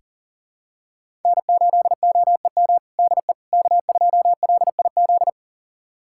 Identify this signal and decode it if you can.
Morse code, heard as a keyed tone, about 40 wpm, 700 Hz. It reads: N8OEM DE K2LIZ
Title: USA Callsign DE Callsign 40wpm